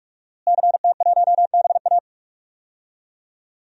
Morse code, heard as a keyed tone, about 46 wpm, 700 Hz. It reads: CT1BA
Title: DXCC-100 Call Signs - 5 Characters - Repeated 3x - 45wpm